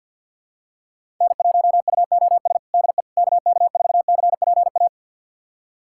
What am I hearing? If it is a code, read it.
Morse code, heard as a keyed tone, about 50 wpm, 700 Hz. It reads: N1UOS DE KK4CPA
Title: USA Callsign DE Callsign 50wpm